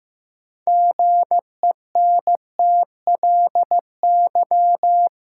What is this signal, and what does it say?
Morse code, heard as a keyed tone, about 15 wpm, 700 Hz. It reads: GENTLY
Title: Sentences from Top 1500 Words - New Word Drill Alt - 15wpm